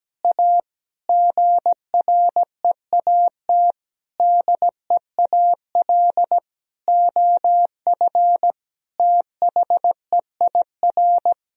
Morse code, heard as a keyed tone, about 17 wpm, 700 Hz. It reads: A GREAT DEAL OF THEIR
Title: Sentences from Top 1000 Words - Review Part 1 - 17wpm